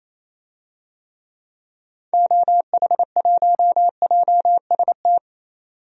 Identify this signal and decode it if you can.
Morse code, heard as a keyed tone, about 28 wpm, 700 Hz. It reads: OH1JST
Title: DXCC-100 Call Signs - 6 Characters - 28wpm